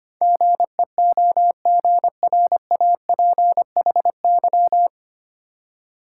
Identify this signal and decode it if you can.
Morse code, heard as a keyed tone, about 25 wpm, 700 Hz. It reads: GEOGRAPHY
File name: practice / 25wpm / 9 Letter Words - First 3 Letters Given - 25wpm